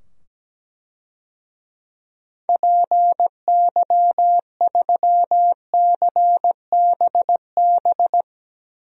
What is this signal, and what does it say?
Morse code, heard as a keyed tone, about 17 wpm, 700 Hz. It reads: PY3CBB
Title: DXCC-100 Call Signs - 6 Characters - 17wpm